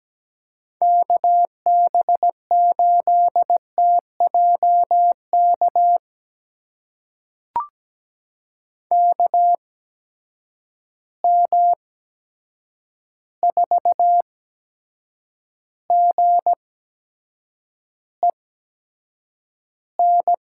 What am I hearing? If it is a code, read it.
Morse code, heard as a keyed tone, about 17 wpm, spaced as for 7 wpm, 700 Hz. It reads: KB8TJK KM4GEN
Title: USA Call Signs - 6 Characters - 8x Character Spacing - 17wpm